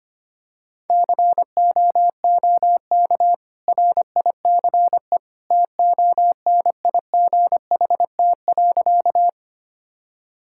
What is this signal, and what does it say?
Morse code, heard as a keyed tone, about 25 wpm, 700 Hz. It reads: COOK RICE TONIGHT.
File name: Easy Sentences with Hints - 25wpm